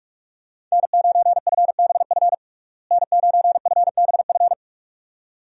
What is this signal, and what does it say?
Morse code, heard as a keyed tone, about 46 wpm, 700 Hz. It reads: N9FBF N9FBF
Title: USA Call Signs - 5 Characters - Repeated 2x - 45wpm